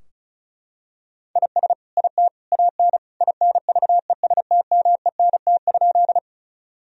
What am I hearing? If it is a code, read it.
Morse code, heard as a keyed tone, about 36 wpm, 700 Hz. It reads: IS IT AN INVESTMENT?
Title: Sentences from Top 1000 Words - Review Part 4 - 35wpm